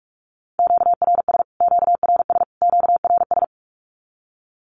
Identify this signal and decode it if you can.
Morse code, heard as a keyed tone, about 44 wpm, 700 Hz. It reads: QRS QRS QRS
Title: Q-Codes Repeated 3x 45wpm